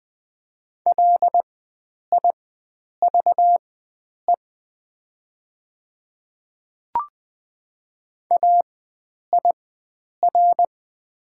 Morse code, heard as a keyed tone, about 20 wpm, spaced as for 9 wpm, 700 Hz. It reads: LIVE AIR
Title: 4x Character Spacing - Top 300 Words - 20wpm